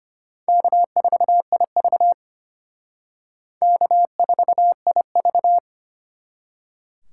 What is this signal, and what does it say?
Morse code, heard as a keyed tone, about 31 wpm, 700 Hz. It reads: K4IV K4IV
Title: Speed Racing - USA 1x2 Call Signs 25wpm